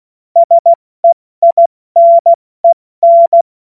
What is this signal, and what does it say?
Morse code, heard as a keyed tone, about 16 wpm, 680 Hz. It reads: SEINEN
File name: Morsekurs / Witze / Witz 27. April 2021, 70 F80